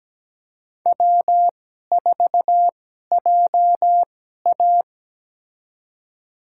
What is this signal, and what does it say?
Morse code, heard as a keyed tone, about 17 wpm, spaced as for 12 wpm, 700 Hz. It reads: W4JA
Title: USA Call Signs - 4 Characters - 2x Character Spacing - 17wpm